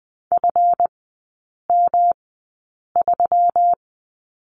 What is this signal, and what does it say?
Morse code, heard as a keyed tone, about 20 wpm, spaced as for 8 wpm, 700 Hz. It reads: FM3
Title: 138 Lesson 23e - Letter f - Three Characters 20wpm